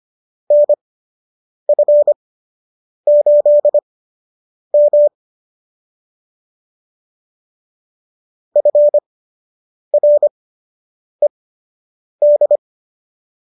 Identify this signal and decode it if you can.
Morse code, heard as a keyed tone, about 25 wpm, spaced as for 8 wpm, 570 Hz. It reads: NF8M FRED